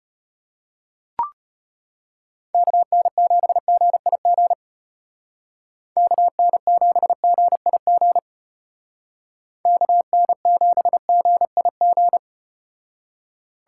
Morse code, heard as a keyed tone, about 33 wpm, 700 Hz. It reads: KN7GIG KN7GIG KN7GIG